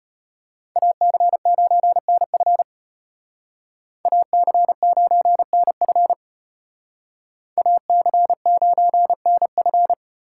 Morse code, heard as a keyed tone, about 32 wpm, 700 Hz. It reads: AC9NF AC9NF AC9NF